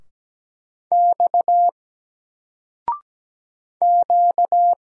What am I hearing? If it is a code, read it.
Morse code, heard as a keyed tone, about 17 wpm, 700 Hz. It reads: X Q
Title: Single Letter-Number 17wpm